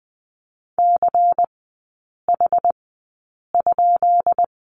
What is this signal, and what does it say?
Morse code, heard as a keyed tone, about 20 wpm, spaced as for 8 wpm, 700 Hz. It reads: CH?
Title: 133 Lesson 22e - Question Mark - Three Characters 20wpm